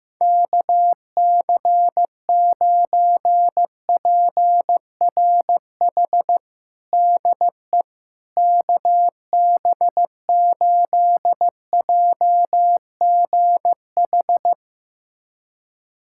Morse code, heard as a keyed tone, about 15 wpm, 700 Hz. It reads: KC9PRH DE KB8JGH
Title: USA Callsign DE Callsign - 6 Characters - 15wpm